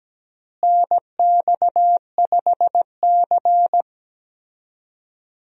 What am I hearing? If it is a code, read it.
Morse code, heard as a keyed tone, about 17 wpm, 700 Hz. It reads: NX5C
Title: USA Call Signs - 4 Characters - Repeated 2x - 17wpm